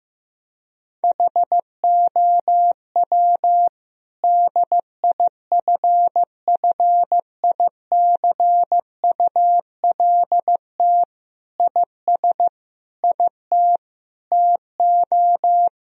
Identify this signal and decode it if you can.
Morse code, heard as a keyed tone, about 14 wpm, 700 Hz. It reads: HOW DIFFICULT IS IT TO
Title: Sentences from Top 1000 Words - Review Part 3 - 15wpm